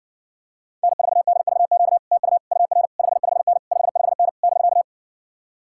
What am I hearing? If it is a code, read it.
Morse code, heard as a keyed tone, about 63 wpm, 700 Hz. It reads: N4DVQ TU UR 55N 55N <BK>